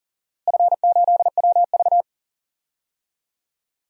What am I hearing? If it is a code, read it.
Morse code, heard as a keyed tone, about 40 wpm, 700 Hz. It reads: F8WV
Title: DXCC-100 Call Signs - 4 Characters - Repeated 3x - 40wpm